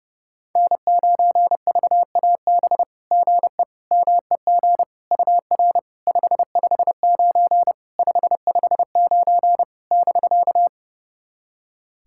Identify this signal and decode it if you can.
Morse code, heard as a keyed tone, about 31 wpm, 700 Hz. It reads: N9VAB GE MEG UR 559 559 <BK>
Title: SOTA QSO Practice 30wpm